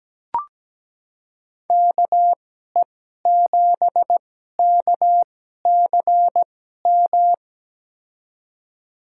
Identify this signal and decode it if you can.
Morse code, heard as a keyed tone, about 17 wpm, spaced as for 12 wpm, 700 Hz. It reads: KE7KCM
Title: USA Call Signs - 2x Character Spacing - 17wpm